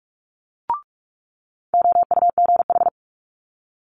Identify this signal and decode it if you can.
Morse code, heard as a keyed tone, about 46 wpm, 700 Hz. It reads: OUGH
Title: Multi-letter Phonemes Quick Fire 45wpm